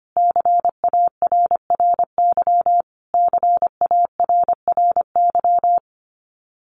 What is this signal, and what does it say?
Morse code, heard as a keyed tone, about 25 wpm, 700 Hz. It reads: CARRY CARRY
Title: Top 500 Words Repeated 25wpm